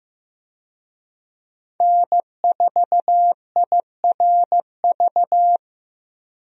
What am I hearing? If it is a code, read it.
Morse code, heard as a keyed tone, about 15 wpm, 700 Hz. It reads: N4IRV
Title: USA Call Signs - 5 Characters - Repeated 2x - 15wpm